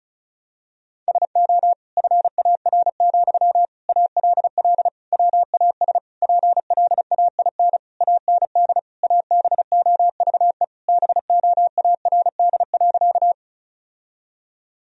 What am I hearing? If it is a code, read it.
Morse code, heard as a keyed tone, about 34 wpm, 700 Hz. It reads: SO FAR, ALL WAS PLAIN AND ABOVE BOARD.
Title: Sentences from Top 1000 Words - Review Part 4 - 35wpm